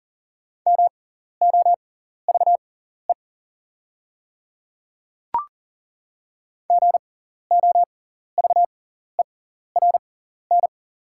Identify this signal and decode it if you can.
Morse code, heard as a keyed tone, about 40 wpm, spaced as for 14 wpm, 690 Hz. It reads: MOVE GOVERN
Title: 6x Character Spacing - Top 500 Words - 40wpm